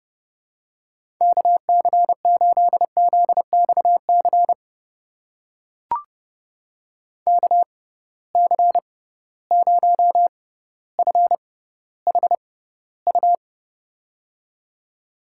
Morse code, heard as a keyed tone, about 30 wpm, spaced as for 15 wpm, 700 Hz. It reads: KC8ZXC KC0FHU
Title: USA Call Signs - 6 Characters - 6x Character Spacing - 30wpm